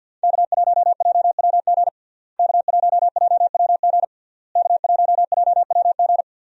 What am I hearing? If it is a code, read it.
Morse code, heard as a keyed tone, about 50 wpm, 700 Hz. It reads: K1JWG K1JWG K1JWG
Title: USA Call Signs - 5 Characters - Repeated 3x - 50wpm